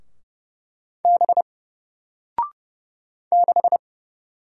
Morse code, heard as a keyed tone, about 30 wpm, 700 Hz. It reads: B 6
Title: Single Character (Letter:Number:Punctuation:Prosign) 30wpm